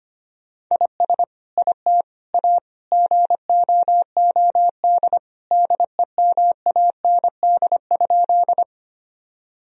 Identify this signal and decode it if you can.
Morse code, heard as a keyed tone, about 25 wpm, 700 Hz. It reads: IS IT A GOOD DEMAND?
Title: Sentences from Top 1000 Words - Review Part 4 - 25wpm